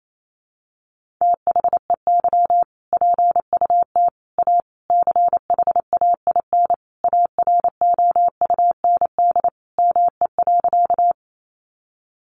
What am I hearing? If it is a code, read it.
Morse code, heard as a keyed tone, about 28 wpm, 700 Hz. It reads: THEY PUT A CHAIN AROUND ME.
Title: Sentences from Top 1500 Words 28wpm